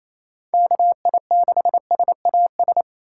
Morse code, heard as a keyed tone, about 28 wpm, 700 Hz. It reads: KI6SAS
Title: USA Call Signs - 6 Characters - 2x Character Spacing - 28wpm